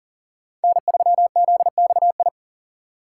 Morse code, heard as a keyed tone, about 40 wpm, 700 Hz. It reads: N3ZXI
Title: USA Call Signs - 5 Characters - 2x Character Spacing - 40wpm